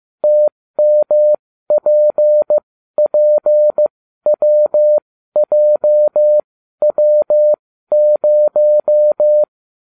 Morse code, heard as a keyed tone, about 15 wpm, spaced as for 12 wpm, 600 Hz. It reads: TMPPWJW0